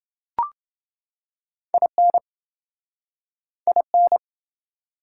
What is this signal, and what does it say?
Morse code, heard as a keyed tone, about 29 wpm, 700 Hz. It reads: IN IN